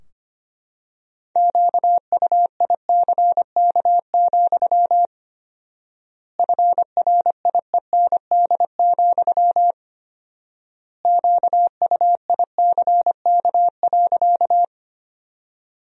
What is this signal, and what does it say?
Morse code, heard as a keyed tone, about 25 wpm, 700 Hz. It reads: QUICK, FRIEND, QUICK.
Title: 4x Word-Spacing - Sentences from Top 500 Words - 25wpm